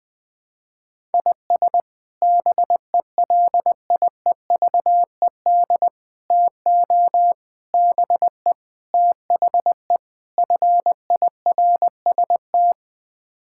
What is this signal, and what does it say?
Morse code, heard as a keyed tone, about 21 wpm, 700 Hz. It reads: IS BELIEVED TO BE THE FIRST
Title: Sets of 6 Words - 20wpm